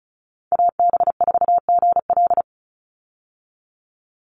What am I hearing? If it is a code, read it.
Morse code, heard as a keyed tone, about 34 wpm, 700 Hz. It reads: AB4GL
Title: USA Call Signs 35wpm